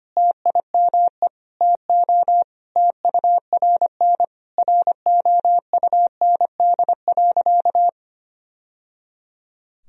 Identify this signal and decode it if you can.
Morse code, heard as a keyed tone, about 25 wpm, 700 Hz. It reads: TIME TO TURN ROUND.
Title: Sentences from Top 1000 Words - Review Part 1 - 25wpm